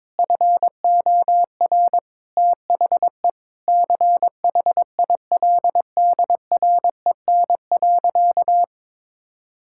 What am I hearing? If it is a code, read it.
Morse code, heard as a keyed tone, about 21 wpm, 700 Hz. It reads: FOR THE CHILDREN.
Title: Sentences from Top 1000 Words - Review Part 5 - 22wpm